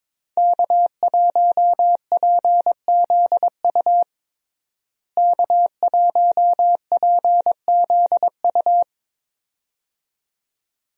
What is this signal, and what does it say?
Morse code, heard as a keyed tone, about 22 wpm, 700 Hz. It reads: K1PZU K1PZU
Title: USA Call Signs - 5 Characters - Repeated 2x - 22wpm